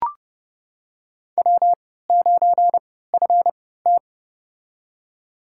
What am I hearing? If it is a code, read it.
Morse code, heard as a keyed tone, about 30 wpm, spaced as for 17 wpm, 700 Hz. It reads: W9FT
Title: USA Call Signs - 3x Character Spacing - 30wpm